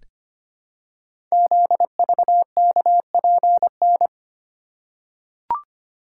Morse code, heard as a keyed tone, about 25 wpm, 700 Hz. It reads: ZVKPN
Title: Code Groups - 5 Letters - 25wpm